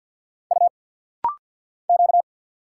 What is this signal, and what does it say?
Morse code, heard as a keyed tone, about 50 wpm, 690 Hz. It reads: U Q
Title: Single Letters - Rapid-Fire 50wpm